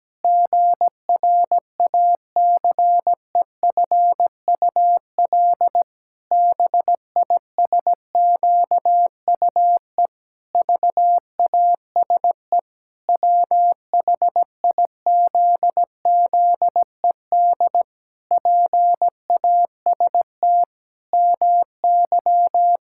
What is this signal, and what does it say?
Morse code, heard as a keyed tone, about 17 wpm, 700 Hz. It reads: GRACEFUL BISQUE VASE WHIZZED PAST MY
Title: Pangram Challenge v2 - 17wpm